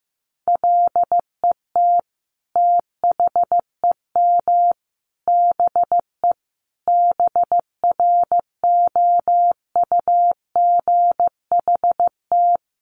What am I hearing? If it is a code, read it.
Morse code, heard as a keyed tone, about 14 wpm, 700 Hz. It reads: LET THEM BE BROUGHT
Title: Sentences from Top 1300 Words 15wpm